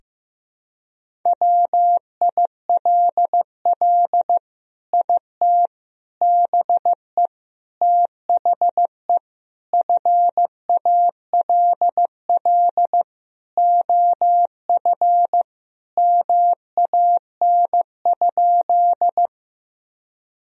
Morse code, heard as a keyed tone, about 15 wpm, 700 Hz. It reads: WILL IT BE THE FALL OF MAN?
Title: Sentences from Top 1000 Words - Review Part 2 - 15wpm